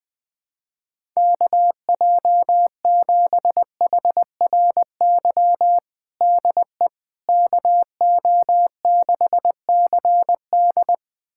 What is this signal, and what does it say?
Morse code, heard as a keyed tone, about 21 wpm, 700 Hz. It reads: KJ7HRY DE KO6CD
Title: USA Callsign DE Callsign 20wpm